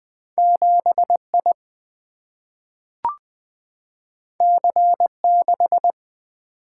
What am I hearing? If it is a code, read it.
Morse code, heard as a keyed tone, about 20 wpm, 700 Hz. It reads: 7 I C 6
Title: Single-Letter Single-Number 20wpm